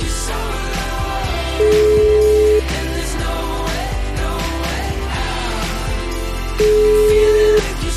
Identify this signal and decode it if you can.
Ringback tone (Europe, 425 Hz), with noise.